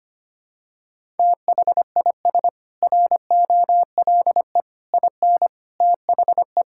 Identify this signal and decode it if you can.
Morse code, heard as a keyed tone, about 25 wpm, 700 Hz. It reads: THIS ROLE IN THE
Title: Sentences from Top 1000 Words - Review Part 6 - 25wpm